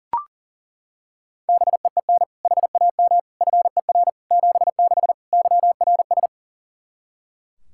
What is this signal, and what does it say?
Morse code, heard as a keyed tone, about 40 wpm, 700 Hz. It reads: BEEN HAM FER 76 YRS
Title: Practice QSO - Exchange 3 - More Info - 40wpm